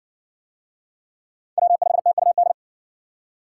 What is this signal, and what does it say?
Morse code, heard as a keyed tone, about 52 wpm, 700 Hz. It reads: W5TUD